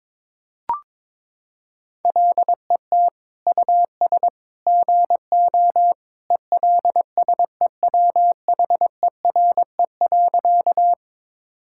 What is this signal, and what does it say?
Morse code, heard as a keyed tone, about 22 wpm, 700 Hz. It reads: LET US GO ELSEWHERE.
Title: Sentences from Top 1600 Words 22wpm